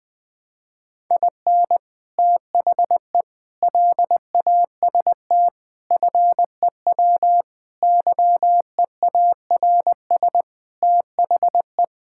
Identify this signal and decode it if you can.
Morse code, heard as a keyed tone, about 20 wpm, 700 Hz. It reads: IN THE LAST FEW YEARS THE
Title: Sentences with 5-Word Phrases - 20wpm